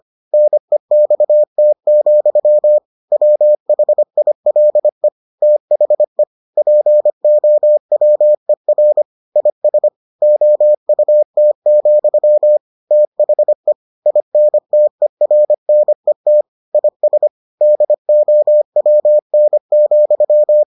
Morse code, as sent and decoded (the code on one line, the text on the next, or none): -. -..- - --..-- / .-- .... .. .-.. / - .... / .--. --- .-- . .-. / .. ... / --- ..- - --..-- / - .... / .. -. - . .-. -. - / .. ... / -.. --- .-- -. --..--
NEXT, WHILE THE POWER IS OUT, THE INTERNET IS DOWN,